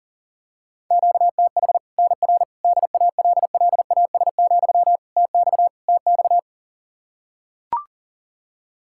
Morse code, heard as a keyed tone, about 40 wpm, 700 Hz. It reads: QTH NR DALLAS, TX TX
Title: Practice QSO - Exchange 1 - Beginning - 40wpm